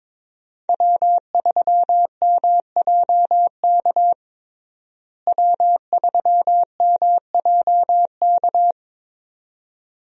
Morse code, heard as a keyed tone, about 22 wpm, 700 Hz. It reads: W3MJK W3MJK
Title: USA Call Signs - 5 Characters - Repeated 2x - 22wpm